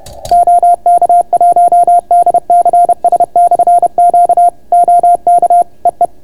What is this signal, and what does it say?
Morse code, heard as a keyed tone, about 30 wpm, 690 Hz. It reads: OK1DCS/Q OK EE